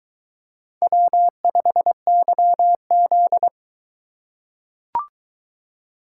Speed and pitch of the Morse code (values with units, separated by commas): 23 wpm, 700 Hz